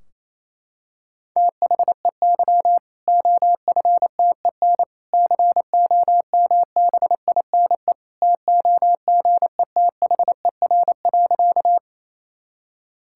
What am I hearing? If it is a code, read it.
Morse code, heard as a keyed tone, about 28 wpm, 700 Hz. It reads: THEY OFTEN COMBINE TOGETHER.
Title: Sentences from Top 1600 Words 28wpm